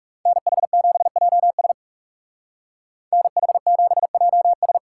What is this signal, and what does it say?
Morse code, heard as a keyed tone, about 36 wpm, 700 Hz. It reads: NH7JS NH7JS
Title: Speed Racing - USA 2x2 Call Signs 30wpm